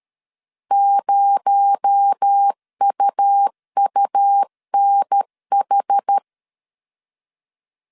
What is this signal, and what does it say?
Morse code, heard as a keyed tone, about 13 wpm, 800 Hz. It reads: 0UUNH
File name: CW / DET / 1 6 1500C S C 12wpm 3533